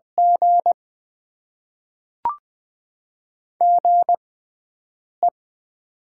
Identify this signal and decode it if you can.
Morse code, heard as a keyed tone, about 20 wpm, spaced as for 7 wpm, 700 Hz. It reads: G GE
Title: Top 20 QSO Elements - 6x Character Spacing - 20wpm